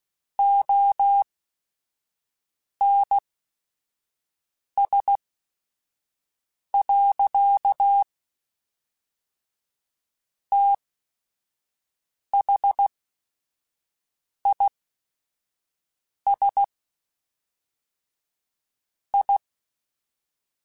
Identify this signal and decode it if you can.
Morse code, heard as a keyed tone, about 16 wpm, spaced as for 5 wpm, 800 Hz. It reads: ONS. THIS I
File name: Technical Subjects / Technical Subjects-mp3s / AND Gate @5